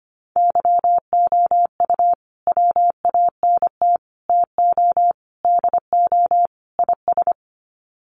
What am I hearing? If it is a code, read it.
Morse code, heard as a keyed tone, about 25 wpm, 700 Hz. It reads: YOU WANT TO DO IS
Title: Sets of 5 Words Encore2 25wpm